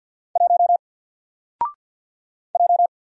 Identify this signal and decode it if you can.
Morse code, heard as a keyed tone, about 50 wpm, 700 Hz. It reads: 1 J
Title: Single Letter-Number Alt 50wpm